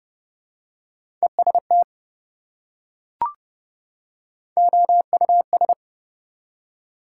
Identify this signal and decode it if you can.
Morse code, heard as a keyed tone, about 30 wpm, 700 Hz. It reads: EST OUS
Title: Prefixes and Suffixes 30wpm